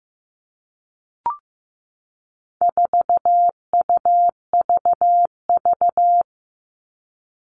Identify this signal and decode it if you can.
Morse code, heard as a keyed tone, about 15 wpm, 700 Hz. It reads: 4UVV
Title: All Cut Numbers - 4-Digit 15wpm